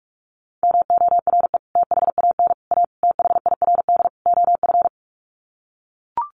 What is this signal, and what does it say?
Morse code, heard as a keyed tone, about 44 wpm, 700 Hz. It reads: MORE THAN A THIRD OF
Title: Sets of 5 Words Encore2 45wpm